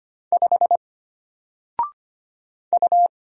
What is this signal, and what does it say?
Morse code, heard as a keyed tone, about 25 wpm, 700 Hz. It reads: 5 U
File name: Single-Letter-Number-25wpm